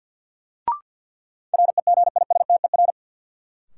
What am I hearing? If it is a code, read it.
Morse code, heard as a keyed tone, about 52 wpm, 700 Hz. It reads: REGISTER
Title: Sentences from Top 1200 Words - New Word Drill - 50wpm